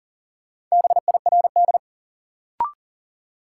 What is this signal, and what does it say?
Morse code, heard as a keyed tone, about 40 wpm, 700 Hz. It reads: BIRD
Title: Sentences from Top 400 Words - New Word Drill - 40wpm